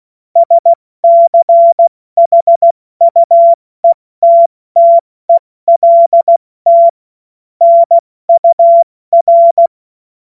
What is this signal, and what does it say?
Morse code, heard as a keyed tone, about 16 wpm, spaced as for 14 wpm, 680 Hz. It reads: SCHUETTELT NUR
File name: Morsekurs / Witze / Witz 02. Februar 2021, 70 F80